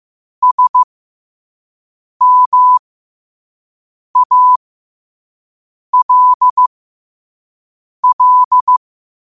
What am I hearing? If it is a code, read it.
Morse code, heard as a keyed tone, about 15 wpm, spaced as for 5 wpm, 1000 Hz. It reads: SMALL
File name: qotdfiles / 5wpm